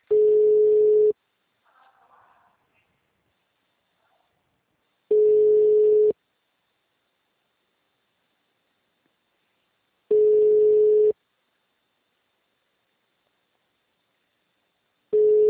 Ringback tone (Europe, 425 Hz).
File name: out- -2012- - - .46495